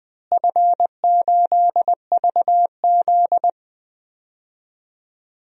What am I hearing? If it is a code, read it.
Morse code, heard as a keyed tone, about 20 wpm, 700 Hz. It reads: F8VZ
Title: DXCC-100 Call Signs - 4 Characters - Repeated 2x - 20wpm